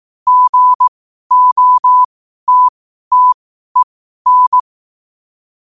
Morse code, heard as a keyed tone, about 18 wpm, spaced as for 12 wpm, 1000 Hz. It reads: GOTTEN
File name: qotdfiles / 12wpm